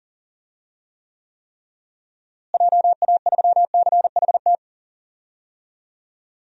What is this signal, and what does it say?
Morse code, heard as a keyed tone, about 38 wpm, 700 Hz. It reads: JA3CHT
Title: DXCC-100 Call Signs - 6 Characters - Repeated 3x - 40wpm